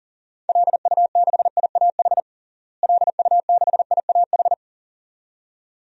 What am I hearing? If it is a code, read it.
Morse code, heard as a keyed tone, about 40 wpm, 700 Hz. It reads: LU6IAH LU6IAH
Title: DXCC-100 Call Signs - 6 Characters - Repeated 2x - 40wpm